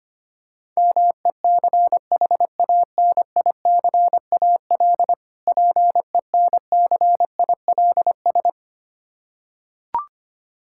Morse code, heard as a keyed tone, about 25 wpm, 700 Hz. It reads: MECHANICAL PENCILS
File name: practice / 25wpm / Difficult Words with Hints - 25wpm